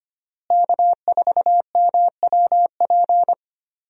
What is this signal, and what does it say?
Morse code, heard as a keyed tone, about 25 wpm, 700 Hz. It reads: K4MWP
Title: USA Call Signs - 5 Characters - Repeated 3x - 25wpm